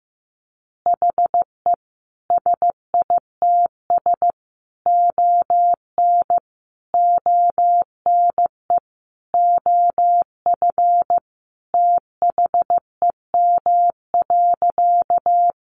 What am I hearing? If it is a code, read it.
Morse code, heard as a keyed tone, about 15 wpm, 700 Hz. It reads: HE SITS ON ONE OF THEM.
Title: Additional Easy Phrases 15wpm